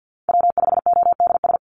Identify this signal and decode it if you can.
Morse code, heard as a keyed tone, about 50 wpm, 700 Hz. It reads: W5ODS
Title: NA SOTA Chaser Callsigns 50wpm